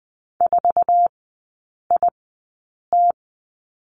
Morse code, heard as a keyed tone, about 20 wpm, spaced as for 8 wpm, 690 Hz. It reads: 4IT
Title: 166 Lesson 27e - Number 7 - Three Characters 20wpm